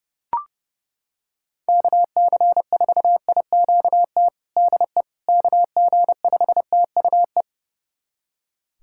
Morse code, heard as a keyed tone, about 29 wpm, 700 Hz. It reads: KC4IQT DE KG5TUE
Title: USA Callsign DE Callsign 30wpm